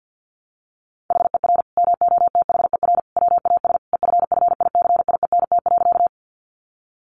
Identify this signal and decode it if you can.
Morse code, heard as a keyed tone, about 50 wpm, 700 Hz. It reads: HER MOTHER WAS EFFICIENT.
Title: Sentences from Top 1300 Words 50wpm